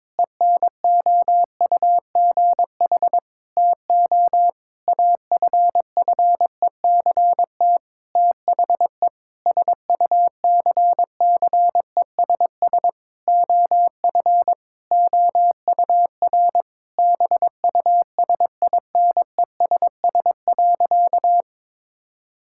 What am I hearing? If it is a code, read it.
Morse code, heard as a keyed tone, about 22 wpm, 700 Hz. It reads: ENOUGH TO AFFECT THE SUCCESS OF OUR BUSINESS.
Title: Sentences from Top 1000 Words - Review Part 6 - 22wpm